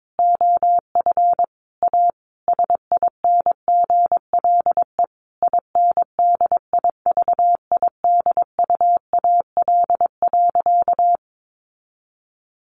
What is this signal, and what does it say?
Morse code, heard as a keyed tone, about 22 wpm, 700 Hz. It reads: OF A SINGLE INDIVIDUAL.
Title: Sentences from Top 1500 Words 22wpm